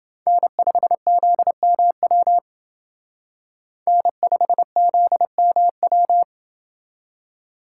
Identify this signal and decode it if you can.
Morse code, heard as a keyed tone, about 28 wpm, 700 Hz. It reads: N5ZMW N5ZMW